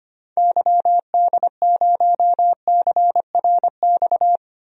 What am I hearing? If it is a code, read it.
Morse code, heard as a keyed tone, about 25 wpm, 700 Hz. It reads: YD0CRX
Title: DXCC-100 Call Signs - 6 Characters - 25wpm